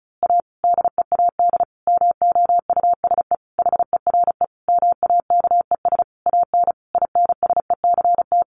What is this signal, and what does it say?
Morse code, heard as a keyed tone, about 34 wpm, 700 Hz. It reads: A DEAD MOUSE HERE MAKES AN INSECT
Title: A Sound Of Thunder 35wpm